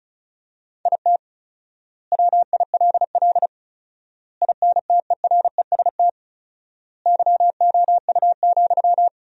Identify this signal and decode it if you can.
Morse code, heard as a keyed tone, about 34 wpm, 700 Hz. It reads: IT WILL INTEREST YOU,
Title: 4x Word-Spacing - Sentences from Top 400 Words - 35wpm